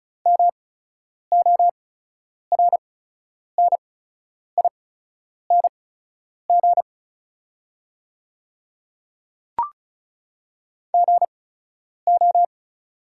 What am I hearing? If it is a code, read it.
Morse code, heard as a keyed tone, about 35 wpm, spaced as for 10 wpm, 700 Hz. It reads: MORNING GO
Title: 8x Character Spacing - Top 500 Words - 35wpm